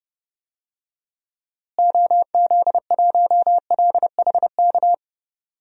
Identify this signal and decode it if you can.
Morse code, heard as a keyed tone, about 32 wpm, 700 Hz. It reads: OZ1LHK
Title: DXCC-100 Call Signs - 6 Characters - Repeated 2x - 30wpm